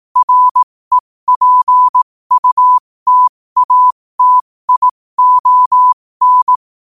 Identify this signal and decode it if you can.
Morse code, heard as a keyed tone, about 18 wpm, spaced as for 15 wpm, 1000 Hz. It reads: REPUTATION
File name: qotdfiles / 15wpm